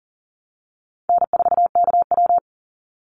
Morse code, heard as a keyed tone, about 39 wpm, 700 Hz. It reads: N4KW
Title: Call Signs - 40wpm